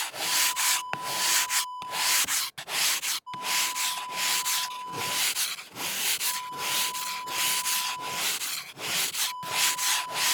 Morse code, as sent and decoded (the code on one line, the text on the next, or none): none